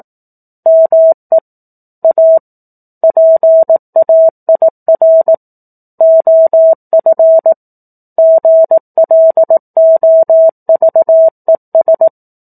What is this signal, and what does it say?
Morse code, heard as a keyed tone, about 18 wpm, 650 Hz. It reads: ME A PAIR OF GLOVES
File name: 04-Chapter-04 18 WPM